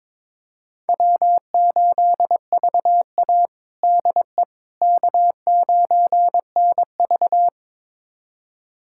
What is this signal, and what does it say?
Morse code, heard as a keyed tone, about 22 wpm, 700 Hz. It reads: W8VA DE K9NV
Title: USA Callsign DE Callsign - 4 Characters - 22wpm